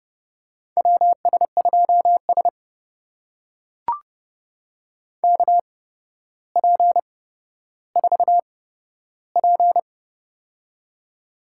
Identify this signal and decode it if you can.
Morse code, heard as a keyed tone, about 30 wpm, spaced as for 13 wpm, 700 Hz. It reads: WS2S KP4P
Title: USA Call Signs - 4 Characters - 8x Character Spacing - 30wpm